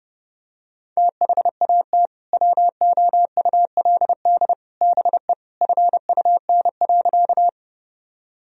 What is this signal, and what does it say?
Morse code, heard as a keyed tone, about 32 wpm, 700 Hz. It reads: THAT WOULD BE FUN.
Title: Sentences from Top 1000 Words - Review Part 4 - 30wpm